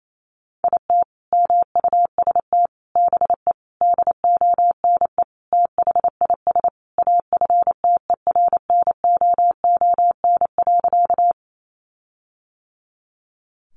Morse code, heard as a keyed tone, about 28 wpm, 700 Hz. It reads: IT MUST BE DONE THIS AFTERNOON.
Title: Sentences from Top 1500 Words 28wpm